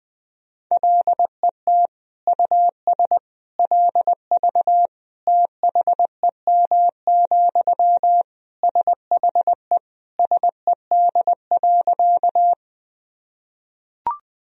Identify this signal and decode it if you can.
Morse code, heard as a keyed tone, about 21 wpm, 700 Hz. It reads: LET US LV THEM, SHE SED.
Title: Sentences from Top 400 Words and Abbreviations 20wpm